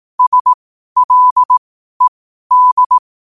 Morse code, heard as a keyed tone, about 18 wpm, spaced as for 12 wpm, 1000 Hz. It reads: SLED
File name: qotdfiles / 12wpm